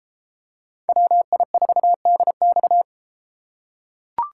Morse code, heard as a keyed tone, about 31 wpm, 700 Hz. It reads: WI4DX